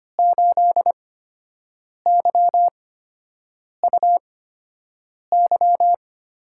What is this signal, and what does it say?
Morse code, heard as a keyed tone, about 25 wpm, spaced as for 7 wpm, 700 Hz. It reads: 8YUY
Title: USA Call Signs - 5 Characters - 8x Character Spacing - 25wpm